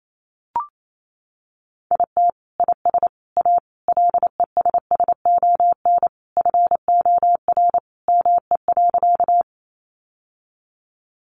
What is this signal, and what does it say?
Morse code, heard as a keyed tone, about 28 wpm, 700 Hz. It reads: IT IS A LESSON FOR ME.
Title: Sentences from Top 1200 Words 28wpm